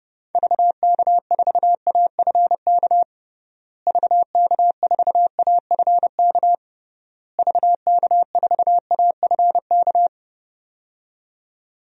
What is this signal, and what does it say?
Morse code, heard as a keyed tone, about 29 wpm, 700 Hz. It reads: VK4AFK VK4AFK VK4AFK
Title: DXCC-100 Call Signs - 6 Characters - Repeated 3x - 30wpm